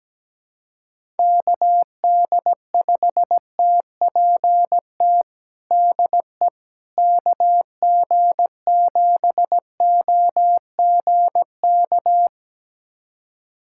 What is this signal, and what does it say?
Morse code, heard as a keyed tone, about 17 wpm, 700 Hz. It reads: KD5TPT DE KG7OGK
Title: USA Callsign DE Callsign 17wpm